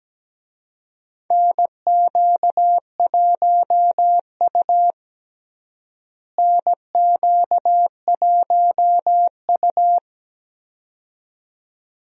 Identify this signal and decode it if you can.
Morse code, heard as a keyed tone, about 17 wpm, 700 Hz. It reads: NQ1U NQ1U
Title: USA Call Signs - 4 Characters - Repeated 2x - 17wpm